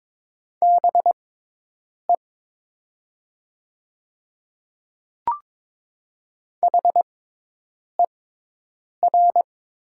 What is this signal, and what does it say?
Morse code, heard as a keyed tone, about 22 wpm, spaced as for 8 wpm, 700 Hz. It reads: BE HER
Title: 6x Character Spacing - Top 100 Words - 22wpm